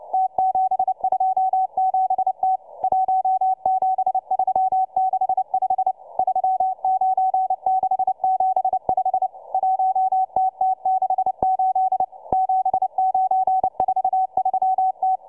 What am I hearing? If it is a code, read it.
Morse code, heard as a keyed tone, about 29 wpm, 760 Hz, with radio noise. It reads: T727T 17365 39675 1TT68 7943T